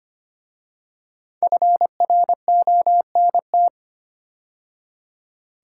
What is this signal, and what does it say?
Morse code, heard as a keyed tone, about 25 wpm, 700 Hz. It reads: FRONT
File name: 5 Letter Words - First 3 Letters Given - 25wpm